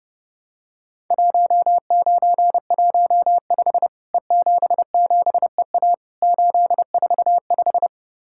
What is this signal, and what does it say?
Morse code, heard as a keyed tone, about 30 wpm, 700 Hz. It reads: 1915 E77EA 845
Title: Worked All Europe DX Contest - QTC Exchange - 30wpm